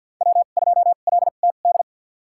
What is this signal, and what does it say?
Morse code, heard as a keyed tone, about 46 wpm, spaced as for 34 wpm, 700 Hz. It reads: W2LTD
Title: USA Call Signs - 2x Character Spacing - 50wpm